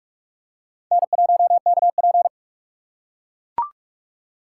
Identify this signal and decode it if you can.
Morse code, heard as a keyed tone, about 44 wpm, 700 Hz. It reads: N1KP